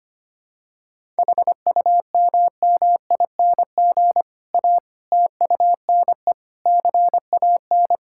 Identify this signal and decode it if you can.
Morse code, heard as a keyed tone, about 25 wpm, 700 Hz. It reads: HUMMING A TUNE CAN
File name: practice / 25wpm / Difficult Sentences with Hints - 25wpm